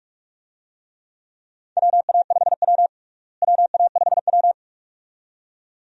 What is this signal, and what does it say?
Morse code, heard as a keyed tone, about 47 wpm, 700 Hz. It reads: WA5W WA5W
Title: USA Call Signs - 4 Characters - Repeated 2x - 45wpm